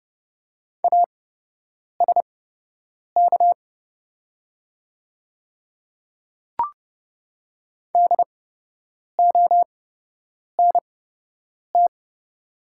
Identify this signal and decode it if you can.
Morse code, heard as a keyed tone, about 31 wpm, spaced as for 8 wpm, 700 Hz. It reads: ASK DONT
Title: 8x Character Spacing - Top 300 Words - 30wpm